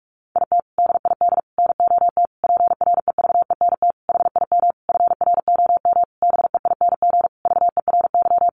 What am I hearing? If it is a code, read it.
Morse code, heard as a keyed tone, about 44 wpm, 700 Hz. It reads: IT DID NOT PREVENT HIM FROM BEING VERY
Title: Sentences from Top 1500 Words 45wpm